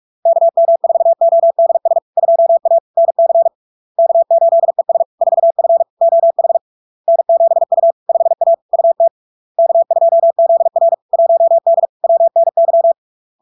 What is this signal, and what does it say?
Morse code, heard as a keyed tone, about 46 wpm, 670 Hz. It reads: KM4ODS 2A NC K8ES 4F OH N7U 5A UT K2ZR 1D WNY
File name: LARK-FD-2022-Log-in-CW-45wpm-670-Hz